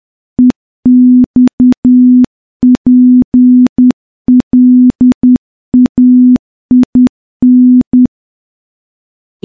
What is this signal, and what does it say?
Morse code, heard as a keyed tone, about 10 wpm, 250 Hz. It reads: EXPLAIN
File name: morse-training-10WPM